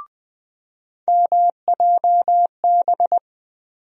Morse code, heard as a keyed tone, about 20 wpm, 700 Hz. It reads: MJB
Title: USA Call Sign Postfixes - 3 Letters 20wpm